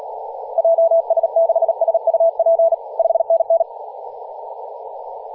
Morse code, heard as a keyed tone, about 38 wpm, 660 Hz, with radio noise. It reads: JS6SUP 5NN
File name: posts / 2016-aug-26-2236z-yj0cow-18-073mhz-cw